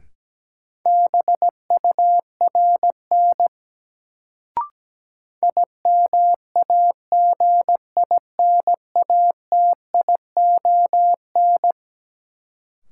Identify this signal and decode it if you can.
Morse code, heard as a keyed tone, about 17 wpm, 700 Hz. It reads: BURN IMAGINATION
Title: Sentences from Top 1300 Words - New Word Drill - 17wpm